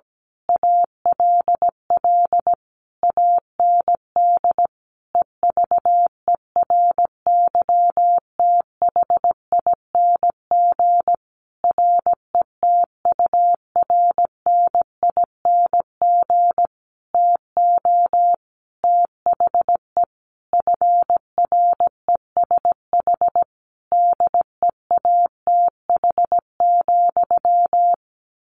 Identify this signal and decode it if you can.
Morse code, heard as a keyed tone, about 17 wpm, 700 Hz. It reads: ALL AND EVERYTHING RETURNING TO THE FRESH DEATH,
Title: A Sound Of Thunder 17wpm